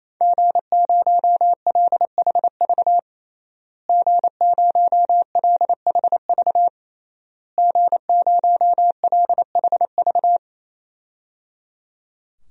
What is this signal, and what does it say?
Morse code, heard as a keyed tone, about 27 wpm, 700 Hz. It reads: G0LHV G0LHV G0LHV
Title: DXCC-100 Call Signs - 5 Characters - Repeated 3x - 28wpm